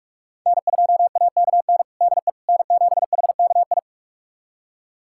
Morse code, heard as a keyed tone, about 46 wpm, 700 Hz. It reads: N2AKN DE N7HKI
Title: USA Callsign DE Callsign - 5 Characters - 45wpm